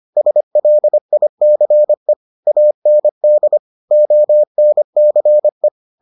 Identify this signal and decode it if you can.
Morse code, heard as a keyed tone, about 25 wpm, 600 Hz. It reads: SLICE AND ONCE